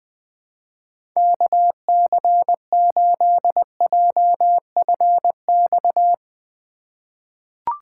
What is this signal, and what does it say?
Morse code, heard as a keyed tone, about 21 wpm, 700 Hz. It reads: KC8JFX
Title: USA Call Signs - 6 Characters - 3x Character Spacing - 20wpm